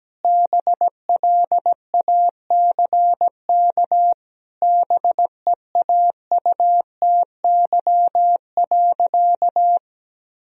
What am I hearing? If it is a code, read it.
Morse code, heard as a keyed tone, about 17 wpm, 700 Hz. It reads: BLACK BEAUTY.
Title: Sentences from Top 1000 Words - Review Part 2 - 17wpm